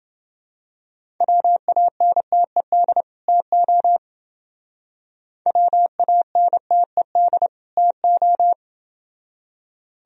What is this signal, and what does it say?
Morse code, heard as a keyed tone, about 28 wpm, 700 Hz. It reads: WANTED TO WANTED TO